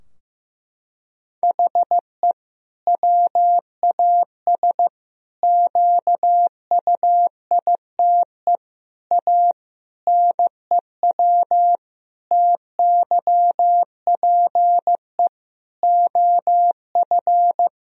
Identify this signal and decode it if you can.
Morse code, heard as a keyed tone, about 16 wpm, 700 Hz. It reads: HE WAS QUITE A NEW TYPE OF
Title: Sentences from Top 1000 Words - Review Part 3 - 15wpm